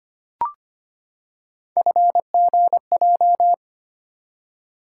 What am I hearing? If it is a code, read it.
Morse code, heard as a keyed tone, about 25 wpm, 700 Hz. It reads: FGJ
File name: Three Letter-Number Combinations - 25wpm